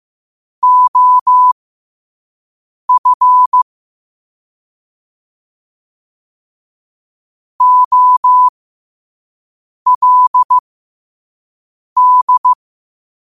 Morse code, heard as a keyed tone, about 15 wpm, spaced as for 5 wpm, 1000 Hz. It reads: OF OLD
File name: qotdfiles / 5wpm